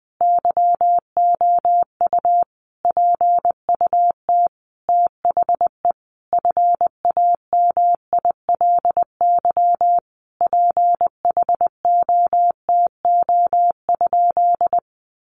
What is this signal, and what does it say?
Morse code, heard as a keyed tone, about 20 wpm, 700 Hz. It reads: YOU PUT THE FAMILY PHOTO?
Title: Sentences from Top 1200 Words 20wpm